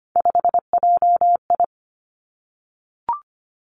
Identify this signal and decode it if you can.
Morse code, heard as a keyed tone, about 25 wpm, 700 Hz. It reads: 5JI
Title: Three Challenging Letter-Number Combinations 25wpm